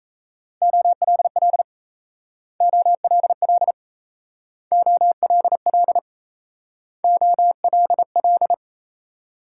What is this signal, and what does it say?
Morse code, heard as a keyed tone, about 36 wpm, 700 Hz. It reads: OLL OLL OLL OLL